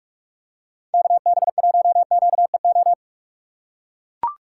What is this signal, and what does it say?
Morse code, heard as a keyed tone, about 44 wpm, 700 Hz. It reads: KB1QEO
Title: USA Call Signs - 6x Character Spacing - 45wpm